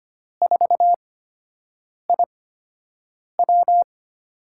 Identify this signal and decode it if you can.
Morse code, heard as a keyed tone, about 25 wpm, spaced as for 7 wpm, 700 Hz. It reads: 4IW
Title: USA Call Signs - 5 Characters - 8x Character Spacing - 25wpm